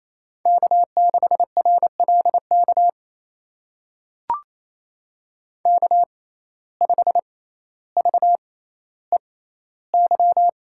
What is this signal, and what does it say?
Morse code, heard as a keyed tone, about 27 wpm, spaced as for 14 wpm, 700 Hz. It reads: K6RLK K5VEY
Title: USA Call Signs - 5 Characters - 6x Character Spacing - 28wpm